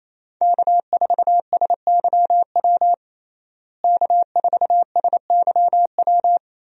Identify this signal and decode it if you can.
Morse code, heard as a keyed tone, about 27 wpm, 700 Hz. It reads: K4SYW K4SYW
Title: USA Call Signs - 5 Characters - Repeated 3x - 28wpm